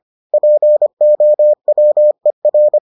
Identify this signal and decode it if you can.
Morse code, heard as a keyed tone, about 25 wpm, 600 Hz. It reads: POWER